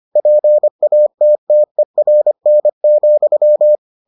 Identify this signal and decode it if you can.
Morse code, heard as a keyed tone, about 25 wpm, 600 Hz. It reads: PATTERN,